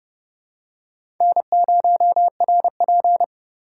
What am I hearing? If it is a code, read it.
Morse code, heard as a keyed tone, about 30 wpm, 700 Hz. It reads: N0RP
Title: USA Call Signs - 6x Character Spacing - 30wpm